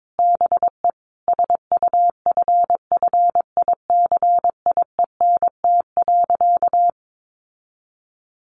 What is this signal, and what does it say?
Morse code, heard as a keyed tone, about 22 wpm, 700 Hz. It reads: BE SUFFICIENT.
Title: Sentences from Top 1400 Words 22wpm